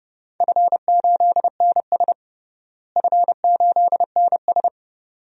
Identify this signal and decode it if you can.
Morse code, heard as a keyed tone, about 31 wpm, 700 Hz. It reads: F8NS F8NS
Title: DXCC-100 Call Signs - 4 Characters - Repeated 2x - 30wpm